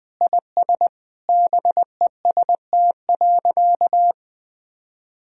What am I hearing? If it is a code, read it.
Morse code, heard as a keyed tone, about 20 wpm, 700 Hz. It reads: IS BEST.
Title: Sentences from Top 1000 Words - Review Part 3 - 20wpm